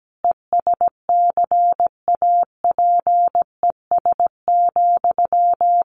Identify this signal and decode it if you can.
Morse code, heard as a keyed tone, about 17 wpm, 700 Hz. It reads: ESCAPES,
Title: A Sound Of Thunder 17wpm